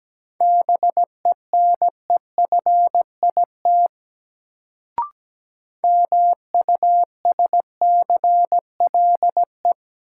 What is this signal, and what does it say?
Morse code, heard as a keyed tone, about 17 wpm, 700 Hz. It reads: BENEFIT MUSCLE
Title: Sentences from Top 1300 Words - New Word Drill Alt - 17wpm